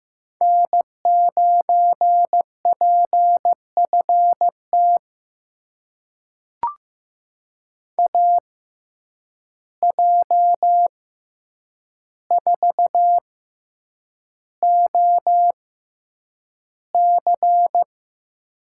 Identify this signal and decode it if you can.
Morse code, heard as a keyed tone, about 15 wpm, spaced as for 8 wpm, 700 Hz. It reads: N9PFT AJ4OC
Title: USA Call Signs - 5 Characters - 6x Character Spacing - 15wpm